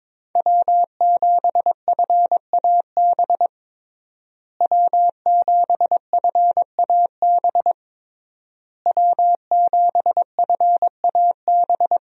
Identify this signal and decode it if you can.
Morse code, heard as a keyed tone, about 22 wpm, 700 Hz. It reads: W7FAB W7FAB W7FAB
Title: USA Call Signs - 5 Characters - Repeated 3x - 22wpm